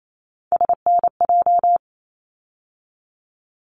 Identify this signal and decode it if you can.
Morse code, heard as a keyed tone, about 28 wpm, 700 Hz. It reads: SNJ
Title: ARRL Sections 28wpm